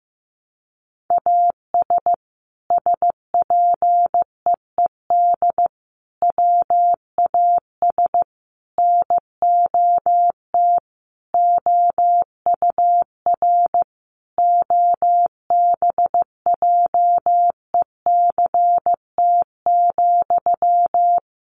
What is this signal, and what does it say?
Morse code, heard as a keyed tone, about 15 wpm, 700 Hz. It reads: AS SPEED WAS NOT OUR OBJECT,
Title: Sentences from Top 800 Words 15wpm